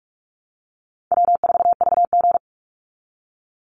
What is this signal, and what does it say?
Morse code, heard as a keyed tone, about 44 wpm, 700 Hz. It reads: W4VG
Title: Call Signs Repeated 3x - 45wpm